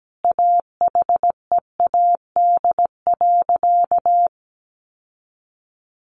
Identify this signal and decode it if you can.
Morse code, heard as a keyed tone, about 17 wpm, 700 Hz. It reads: AHEAD.
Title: Sentences from Top 1400 Words 17wpm